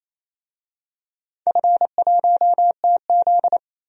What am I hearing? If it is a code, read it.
Morse code, heard as a keyed tone, about 28 wpm, 700 Hz. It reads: F1TZ
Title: DXCC-100 Call Signs - 4 Characters - 28wpm